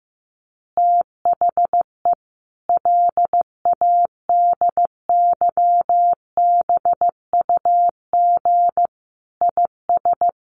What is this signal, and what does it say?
Morse code, heard as a keyed tone, about 15 wpm, 700 Hz. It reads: THE LADYBUG IS
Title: Additional Easy Phrases 15wpm